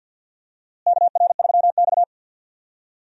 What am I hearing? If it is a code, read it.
Morse code, heard as a keyed tone, about 53 wpm, 700 Hz. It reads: KR3X
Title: USA Call Signs - 4 Characters - Repeated 2x - 50wpm